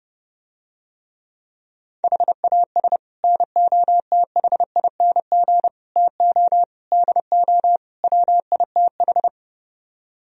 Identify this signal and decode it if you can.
Morse code, heard as a keyed tone, about 30 wpm, 700 Hz. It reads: HAS NOTHING TO DO WITH